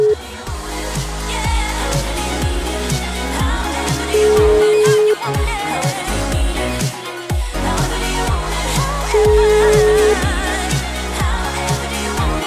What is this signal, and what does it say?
Ringback tone (Europe, 425 Hz), with noise.